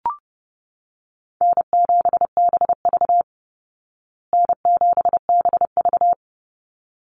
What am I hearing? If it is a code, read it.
Morse code, heard as a keyed tone, about 30 wpm, 700 Hz. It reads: N7BV N7BV
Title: Easy Call Signs Repeated 2x - 30wpm